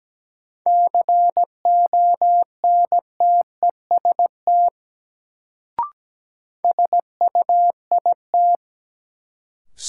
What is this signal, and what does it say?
Morse code, heard as a keyed tone, about 17 wpm, 700 Hz. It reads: CONTEST SUIT
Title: Sentences from Top 1600 Words - New Word Drill - 17wpm